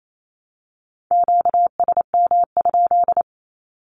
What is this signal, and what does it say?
Morse code, heard as a keyed tone, about 28 wpm, 700 Hz. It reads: QSM?
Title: Q-Codes 28wpm